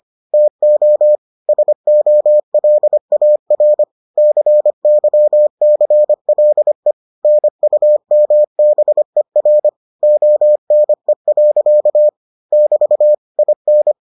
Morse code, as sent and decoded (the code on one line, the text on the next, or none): - --- / ... --- .-.. .- .-. / -.-. -.-- -.-. .-.. / -. ..- -- -... .-. / --- -. .-.-.- / -...- / .. -.
TO SOLAR CYCLE NUMBER ONE. <BT> IN